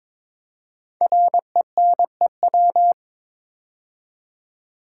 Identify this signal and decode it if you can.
Morse code, heard as a keyed tone, about 22 wpm, 700 Hz. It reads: RENEW
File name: 5 Letter Words - First 3 Letters Given - 22wpm